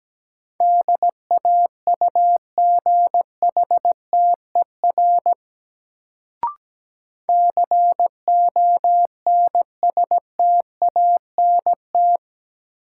Morse code, heard as a keyed tone, about 17 wpm, 700 Hz. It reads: DAUGHTER CONSTANT
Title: Sentences from Top 1500 Words - New Word Drill - 17wpm